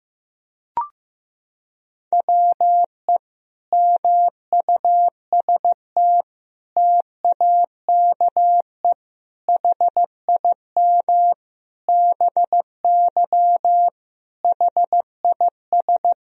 Morse code, heard as a keyed tone, about 15 wpm, 700 Hz. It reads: WE MUST TAKE HIM BY HIS
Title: Sentences from Top 1000 Words - Review Part 1 - 15wpm